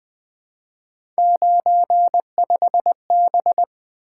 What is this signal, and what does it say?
Morse code, heard as a keyed tone, about 20 wpm, 700 Hz. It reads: 95B
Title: Three Letter-Number Combinations - 20wpm